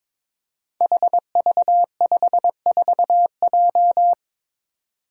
Morse code, heard as a keyed tone, about 22 wpm, 700 Hz. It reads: HV54J
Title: Five Challenging Letter-Number Combinations 22wpm